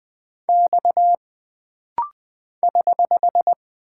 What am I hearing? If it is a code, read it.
Morse code, heard as a keyed tone, about 20 wpm, 700 Hz. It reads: X <HH>
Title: Single Character (Letter:Number:Punctuation:Prosign) - Rapid-Fire - 20wpm